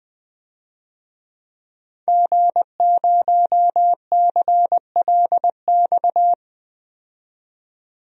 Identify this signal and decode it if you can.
Morse code, heard as a keyed tone, about 20 wpm, 700 Hz. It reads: G0CLX
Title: DXCC-100 Call Signs - 5 Characters - 20wpm